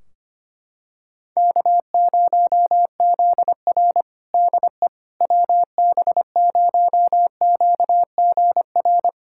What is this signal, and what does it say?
Morse code, heard as a keyed tone, about 25 wpm, 700 Hz. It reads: K0ZR DE WB0QGR
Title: Practice QSO - Exchange 4 - Ending - 25wpm